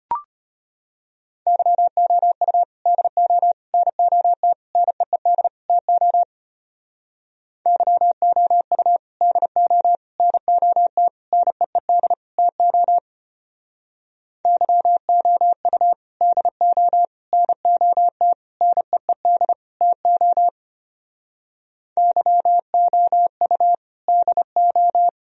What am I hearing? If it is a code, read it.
Morse code, heard as a keyed tone, about 33 wpm, 700 Hz. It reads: YOU DO NOT NEED TO YOU DO NOT NEED TO YOU DO NOT NEED TO YOU DO